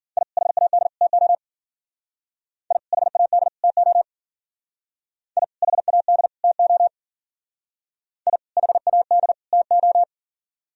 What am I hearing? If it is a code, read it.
Morse code, heard as a keyed tone, about 51 wpm, 700 Hz. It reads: I HAD TO I HAD TO I HAD TO I HAD TO